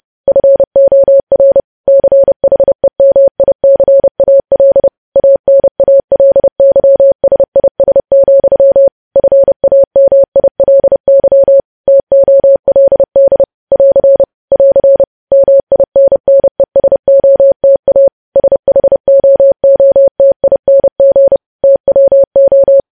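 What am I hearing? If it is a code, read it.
Morse code, heard as a keyed tone, about 30 wpm, 550 Hz. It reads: FOR CHEMICAL ANALYSIS, FAMILY TOLD <AR> <AR> MINNESOTA SHOOTING TWO
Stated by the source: Morse code news headlines sent at 30 WPM.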